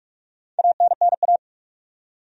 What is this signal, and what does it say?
Morse code, heard as a keyed tone, about 46 wpm, 700 Hz. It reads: ANNA
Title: USA Common Names - 45wpm